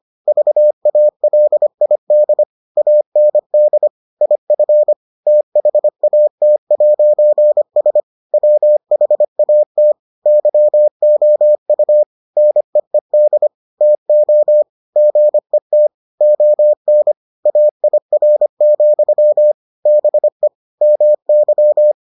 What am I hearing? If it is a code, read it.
Morse code, heard as a keyed tone, about 25 wpm, 600 Hz. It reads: VALID AND IF THAT'S WHAT YOU NEED TO GET ON AIR, BE MY